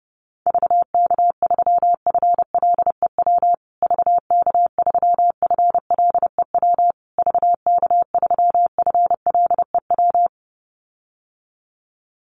Morse code, heard as a keyed tone, about 30 wpm, 700 Hz. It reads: VK3FLEW VK3FLEW VK3FLEW
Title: USA Call Signs Repeated 3x 30wpm